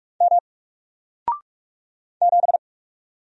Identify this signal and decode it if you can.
Morse code, heard as a keyed tone, about 44 wpm, 700 Hz. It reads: M 7
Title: Single Letter-Number Alt 45wpm